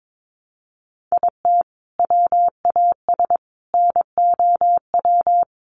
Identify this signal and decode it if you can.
Morse code, heard as a keyed tone, about 22 wpm, 700 Hz. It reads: IT WAS NOW
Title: Sentences from Top 1100 Words 22wpm